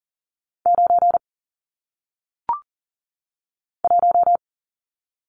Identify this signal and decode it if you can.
Morse code, heard as a keyed tone, about 40 wpm, 700 Hz. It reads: 9 1
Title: Numbers - 1-Digit 40wpm